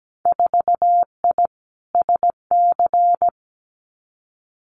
Morse code, heard as a keyed tone, about 17 wpm, 700 Hz. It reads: 4I SC
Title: Winter Field Day Exchanges 17wpm